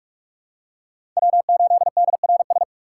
Morse code, heard as a keyed tone, about 44 wpm, 700 Hz. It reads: W8DRS
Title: USA Call Signs - 5 Characters - 4x Character Spacing - 45wpm